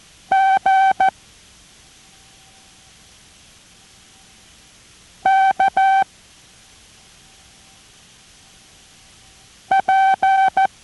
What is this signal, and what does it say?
Morse code, heard as a keyed tone, about 14 wpm, 770 Hz. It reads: G K P